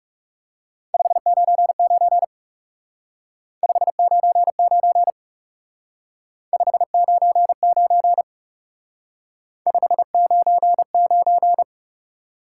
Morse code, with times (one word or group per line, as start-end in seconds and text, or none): 0.9-2.2 599
3.6-5.1 599
6.5-8.2 599
9.7-11.6 599